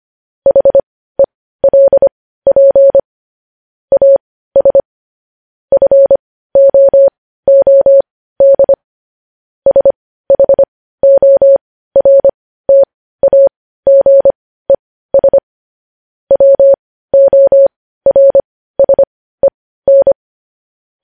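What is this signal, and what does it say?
Morse code, heard as a keyed tone, about 25 wpm, spaced as for 15 wpm, 550 Hz. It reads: HELP AS FOOD SHORTAGES WORSEN
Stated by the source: Morse code transcription of news headlines.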